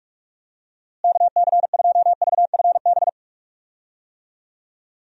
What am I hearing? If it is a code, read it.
Morse code, heard as a keyed tone, about 46 wpm, 700 Hz. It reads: KC2VFB
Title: USA Call Signs - 6 Characters - Repeated 2x - 45wpm